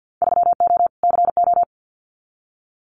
Morse code, heard as a keyed tone, about 50 wpm, 700 Hz. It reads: 3O CO
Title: Winter Field Day Exchanges Encore 50wpm